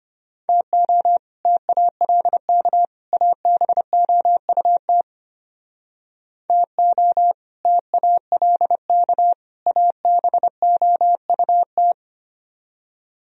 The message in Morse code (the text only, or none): TO TALK ABOUT TO TALK ABOUT